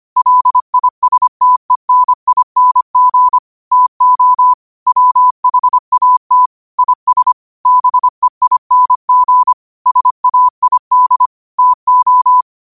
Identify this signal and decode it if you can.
Morse code, heard as a keyed tone, about 25 wpm, 1000 Hz. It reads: LISTENING TO WHAT IS BEING SAID TO
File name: qotdfiles / 25wpm